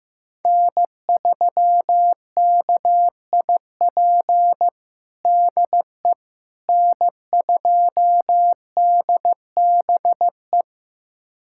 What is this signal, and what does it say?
Morse code, heard as a keyed tone, about 16 wpm, 700 Hz. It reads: N3KIP DE N2DBE
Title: USA Callsign DE Callsign - 5 Characters - 15wpm